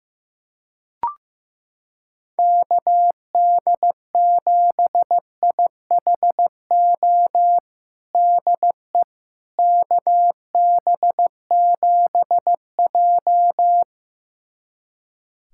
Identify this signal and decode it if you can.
Morse code, heard as a keyed tone, about 15 wpm, 700 Hz. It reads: KD7IHO DE KB7J
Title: USA Callsign DE Callsign 15wpm